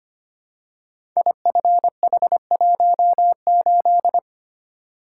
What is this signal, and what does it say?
Morse code, heard as a keyed tone, about 25 wpm, 700 Hz. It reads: IFH18
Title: Code Groups - 5 Letter:Numbers - 25wpm